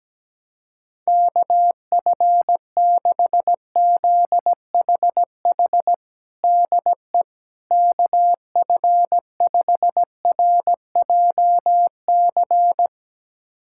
Morse code, heard as a keyed tone, about 17 wpm, 700 Hz. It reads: KF6ZHH DE KF5RJC
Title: USA Callsign DE Callsign 17wpm